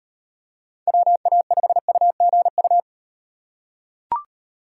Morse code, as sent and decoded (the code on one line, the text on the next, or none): .-- .- ..... ..- --. ..-
WA5UGU